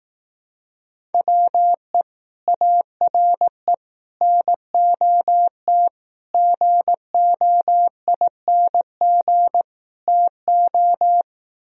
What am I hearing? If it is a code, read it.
Morse code, heard as a keyed tone, about 18 wpm, 700 Hz. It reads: WE ARE NOT GOING TO
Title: Speed-Racing - Sets of 5 Words 15wpm